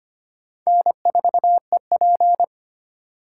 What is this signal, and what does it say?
Morse code, heard as a keyed tone, about 30 wpm, 700 Hz. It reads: N4EP
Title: Speed Racing - USA 1x2 Call Signs 25wpm